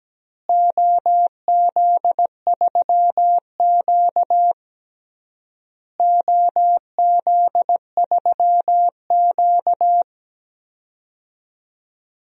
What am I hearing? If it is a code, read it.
Morse code, heard as a keyed tone, about 17 wpm, 700 Hz. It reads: OZ3Q OZ3Q
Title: DXCC-100 Call Signs - 4 Characters - Repeated 3x - 17wpm